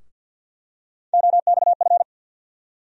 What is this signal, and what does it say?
Morse code, heard as a keyed tone, about 47 wpm, 700 Hz. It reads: OXF
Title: USA Call Sign Postfixes - 3 Letters 50wpm